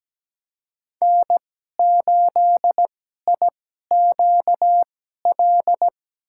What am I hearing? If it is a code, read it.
Morse code, heard as a keyed tone, about 17 wpm, spaced as for 12 wpm, 700 Hz. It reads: N8IQL
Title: USA Call Signs - 2x Character Spacing - 17wpm